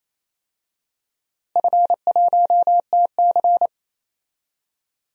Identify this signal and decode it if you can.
Morse code, heard as a keyed tone, about 27 wpm, 700 Hz. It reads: F1TC
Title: DXCC-100 Call Signs - 4 Characters - Repeated 2x - 28wpm